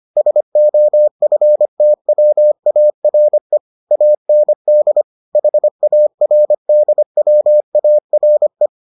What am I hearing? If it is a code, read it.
Morse code, heard as a keyed tone, about 25 wpm, 600 Hz. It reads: SOFTWARE AND HARDWARE